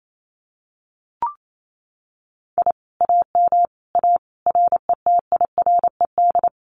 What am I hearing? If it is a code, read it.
Morse code, heard as a keyed tone, about 28 wpm, 700 Hz. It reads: I AM A RETIRED
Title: Common QSO Phrases Encore 28wpm